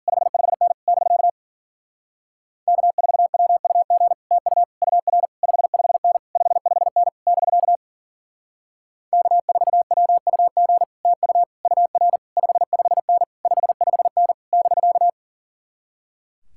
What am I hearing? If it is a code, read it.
Morse code, heard as a keyed tone, about 52 wpm, 700 Hz. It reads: 55N <BK> K4WUG TU UR 55N 55N <BK> K4WUG TU UR 55N 55N <BK>